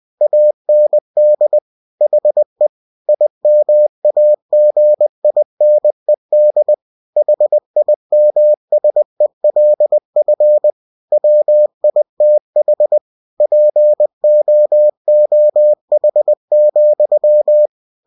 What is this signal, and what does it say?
Morse code, heard as a keyed tone, about 20 wpm, 600 Hz. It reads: AND HE IMAGINED HIMSELF WITH POOH,
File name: cw-books / winnie-the-pooh / winnie-20wpm-page-0091